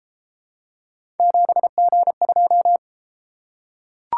Morse code, heard as a keyed tone, about 31 wpm, 700 Hz. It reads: 7G2